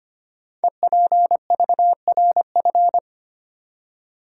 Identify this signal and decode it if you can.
Morse code, heard as a keyed tone, about 25 wpm, 700 Hz. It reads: EPVRF
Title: Code Groups - 5 Letters - 25wpm